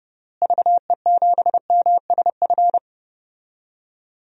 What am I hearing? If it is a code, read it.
Morse code, heard as a keyed tone, about 30 wpm, 700 Hz. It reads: VE7MSF
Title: DXCC-100 Call Signs - 6 Characters - Repeated 2x - 30wpm